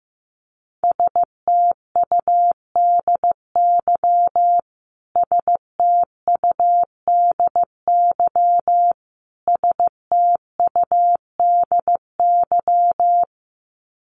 Morse code, heard as a keyed tone, about 15 wpm, 700 Hz. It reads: STUDY STUDY STUDY
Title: Top 500 Words Repeated 15wpm